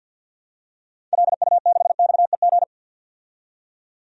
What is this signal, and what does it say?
Morse code, heard as a keyed tone, about 53 wpm, 700 Hz. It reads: LU6XEG
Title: DXCC-100 Call Signs - 6 Characters - 50wpm